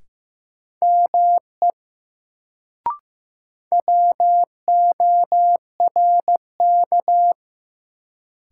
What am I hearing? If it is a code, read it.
Morse code, heard as a keyed tone, about 15 wpm, 700 Hz. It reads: ME WORK
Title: Sentences from Top 200 Words - New Word Drill - 15wpm